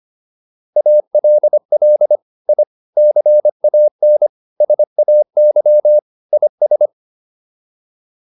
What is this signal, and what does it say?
Morse code, heard as a keyed tone, about 25 wpm, 600 Hz. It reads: ALL I CAN SAY IS